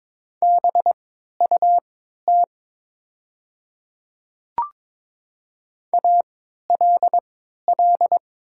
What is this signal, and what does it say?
Morse code, heard as a keyed tone, about 22 wpm, spaced as for 12 wpm, 690 Hz. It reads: BUT ALL
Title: 3x Character Spacing - Top 200 Words - 22wpm